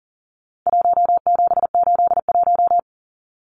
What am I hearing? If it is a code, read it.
Morse code, heard as a keyed tone, about 40 wpm, 700 Hz. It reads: 1781
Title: Numbers - Contest Style - 4-Digit 40wpm